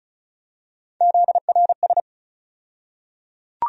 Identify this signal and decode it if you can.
Morse code, heard as a keyed tone, about 36 wpm, 700 Hz. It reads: ZRS
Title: USA Call Sign Postfixes - 3 Letters 35wpm